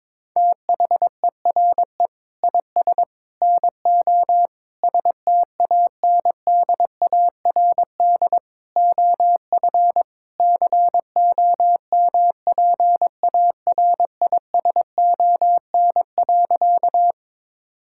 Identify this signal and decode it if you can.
Morse code, heard as a keyed tone, about 22 wpm, 700 Hz. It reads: THERE IS NO STANDARD OF COMPARISON.
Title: Sentences from Top 1600 Words 22wpm